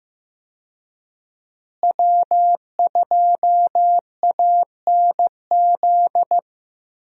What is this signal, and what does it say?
Morse code, heard as a keyed tone, about 15 wpm, 700 Hz. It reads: W2ANZ
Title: USA Call Signs - 5 Characters - Repeated 3x - 15wpm